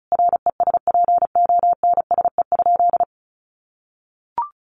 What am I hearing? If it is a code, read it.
Morse code, heard as a keyed tone, about 34 wpm, 700 Hz. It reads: RESPONSE?
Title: Sentences from Top 1400 Words 35wpm